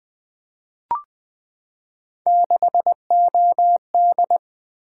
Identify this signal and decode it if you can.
Morse code, heard as a keyed tone, about 20 wpm, 700 Hz. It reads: 6OD
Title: Three Letter-Number Combinations - 20wpm